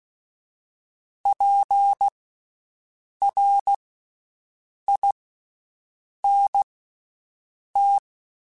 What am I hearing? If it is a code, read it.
Morse code, heard as a keyed tone, about 16 wpm, spaced as for 6 wpm, 800 Hz. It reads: PRINT
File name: Special INGs-mp3s / Special INGs-mp3s / Slow ing @15